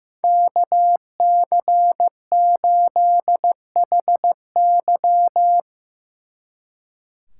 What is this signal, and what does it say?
Morse code, heard as a keyed tone, about 16 wpm, 700 Hz. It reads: KC8HY
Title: Speed Racing - USA 2x2 Call Signs 15wpm